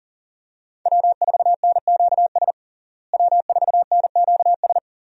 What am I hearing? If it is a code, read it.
Morse code, heard as a keyed tone, about 37 wpm, 700 Hz. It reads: W4NQS W4NQS
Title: USA Call Signs - 5 Characters - Repeated 2x - 40wpm